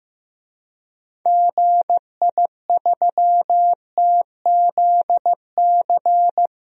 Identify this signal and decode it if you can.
Morse code, heard as a keyed tone, about 15 wpm, 700 Hz. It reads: GI3TZC
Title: DXCC-100 Call Signs - 6 Characters - 15wpm